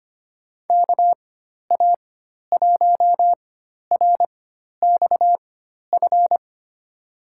Morse code, heard as a keyed tone, about 25 wpm, spaced as for 12 wpm, 700 Hz. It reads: KA1RXF
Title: USA Call Signs - 4x Character Spacing - 25wpm